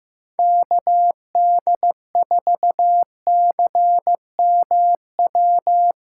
Morse code, heard as a keyed tone, about 15 wpm, 700 Hz. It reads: KD4CMW
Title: USA Callsign DE Callsign 15wpm